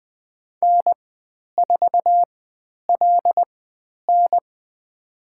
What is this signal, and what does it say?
Morse code, heard as a keyed tone, about 20 wpm, spaced as for 10 wpm, 700 Hz. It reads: N4LN
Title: 102 Lesson 17g - Letter u - Call Signs 20wpm